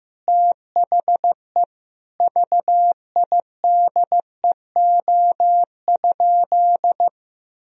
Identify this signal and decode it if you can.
Morse code, heard as a keyed tone, about 15 wpm, 700 Hz. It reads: THE VIDEO?
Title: Sentences from Top 1000 Words - Review Part 4 - 15wpm